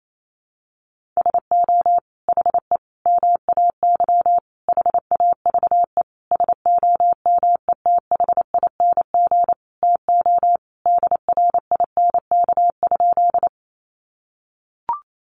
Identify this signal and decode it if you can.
Morse code, heard as a keyed tone, about 28 wpm, 700 Hz. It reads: SO HE MAY HAVE SOMETHING TO DRINK?
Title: Sentences from Top 1500 Words 28wpm